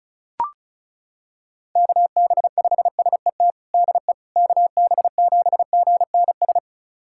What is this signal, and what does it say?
Morse code, heard as a keyed tone, about 34 wpm, 700 Hz. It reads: KB5SET DE KB7GNS
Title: USA Callsign DE Callsign - 6 Characters - 35wpm